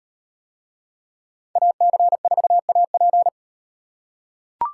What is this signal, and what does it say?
Morse code, heard as a keyed tone, about 38 wpm, 700 Hz. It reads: AC4AP